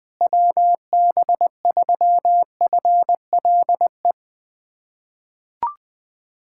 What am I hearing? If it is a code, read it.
Morse code, heard as a keyed tone, about 20 wpm, 700 Hz. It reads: WB3FLE
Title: USA Callsign DE Callsign 20wpm